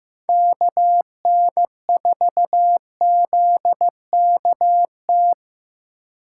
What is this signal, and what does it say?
Morse code, heard as a keyed tone, about 15 wpm, 700 Hz. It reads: KN4ZKT
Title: K1USN SST QSO Practice - 15wpm